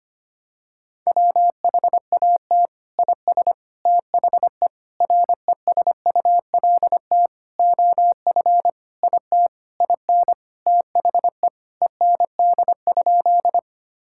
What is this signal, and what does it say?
Morse code, heard as a keyed tone, about 25 wpm, 700 Hz. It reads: WHAT IS THE RESULT OF IT IN THE END?
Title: Sentences from Top 1000 Words - Review Part 5 - 25wpm